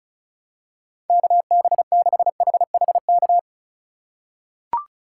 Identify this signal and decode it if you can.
Morse code, heard as a keyed tone, about 36 wpm, 700 Hz. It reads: KB6HHK
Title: USA Call Signs - 2x Character Spacing - 35wpm